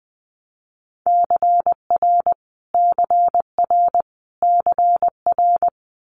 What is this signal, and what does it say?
Morse code, heard as a keyed tone, about 20 wpm, 700 Hz. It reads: CR CR CR
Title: Multi-letter Phonemes Repeated 3x 20wpm